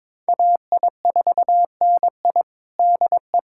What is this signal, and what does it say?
Morse code, heard as a keyed tone, about 22 wpm, 700 Hz. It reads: AI4NI DE
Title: USA Callsign DE Callsign - 5 Characters - 22wpm